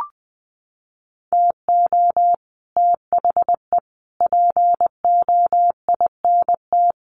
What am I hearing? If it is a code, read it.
Morse code, heard as a keyed tone, about 20 wpm, 700 Hz. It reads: TO THE POINT
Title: Sets of 4 Words 20wpm